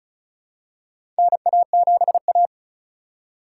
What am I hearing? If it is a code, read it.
Morse code, heard as a keyed tone, about 34 wpm, 700 Hz. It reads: NA7A
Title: USA Call Signs - 4 Characters - 2x Character Spacing - 35wpm